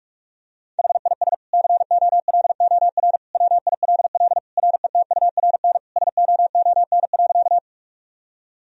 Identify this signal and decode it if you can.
Morse code, heard as a keyed tone, about 47 wpm, 700 Hz. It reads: HIS COLOR WILL RETURN SOON.
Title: Sentences from Top 1000 Words - Review Part 2 - 45wpm